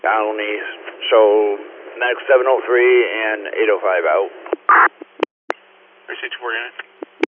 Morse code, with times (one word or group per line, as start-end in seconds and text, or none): none